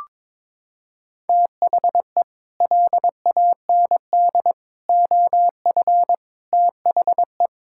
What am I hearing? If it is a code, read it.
Morse code, heard as a keyed tone, about 22 wpm, 700 Hz. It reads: THE LAND OF THE
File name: practice / 22wpm / Sets of 8 Words - 22wpm